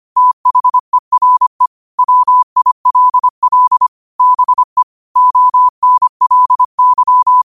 Morse code, heard as a keyed tone, about 25 wpm, 1000 Hz. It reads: THERE WILL BE ONLY
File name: qotdfiles / 25wpm